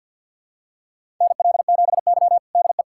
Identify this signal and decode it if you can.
Morse code, heard as a keyed tone, about 47 wpm, 700 Hz. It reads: NL7Y DE
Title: USA Callsign DE Callsign - 4 Characters - 50wpm